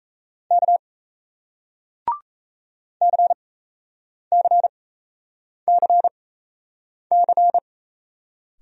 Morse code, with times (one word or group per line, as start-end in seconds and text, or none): 0.5-0.8 K
3.0-7.6 CCCC